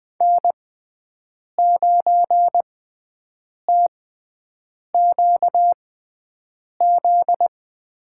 Morse code, heard as a keyed tone, about 20 wpm, spaced as for 7 wpm, 700 Hz. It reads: N9TQZ
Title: USA Call Signs - 5 Characters - 6x Character Spacing - 20wpm